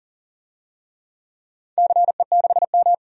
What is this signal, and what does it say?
Morse code, heard as a keyed tone, about 40 wpm, 700 Hz. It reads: CE6M
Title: DXCC-100 Call Signs - 4 Characters - 40wpm